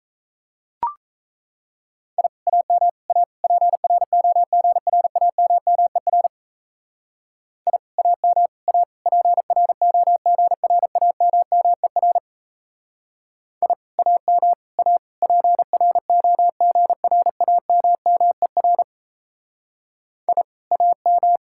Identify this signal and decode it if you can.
Morse code, heard as a keyed tone, about 36 wpm, 700 Hz. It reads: I AM A PROGRAMMER I AM A PROGRAMMER I AM A PROGRAMMER I AM